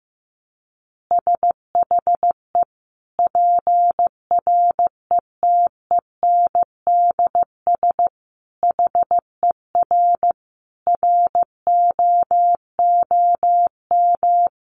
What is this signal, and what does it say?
Morse code, heard as a keyed tone, about 15 wpm, 700 Hz. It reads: SHE PRETENDS HER ROOM
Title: Additional Easy Phrases 15wpm